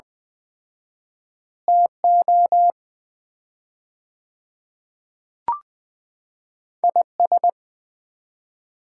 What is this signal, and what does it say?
Morse code, heard as a keyed tone, about 21 wpm, 700 Hz. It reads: TO IS
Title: 4x Word-Spacing - Sets of 3 Words - 20wpm